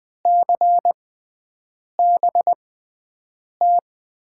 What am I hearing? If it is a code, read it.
Morse code, heard as a keyed tone, about 20 wpm, spaced as for 7 wpm, 700 Hz. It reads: CBT
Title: USA Call Signs - 6 Characters - 6x Character Spacing - 20wpm